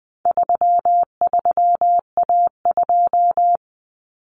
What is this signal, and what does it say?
Morse code, heard as a keyed tone, about 20 wpm, 700 Hz. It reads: 33A2
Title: Common Cut Numbers - 4-Digit 20wpm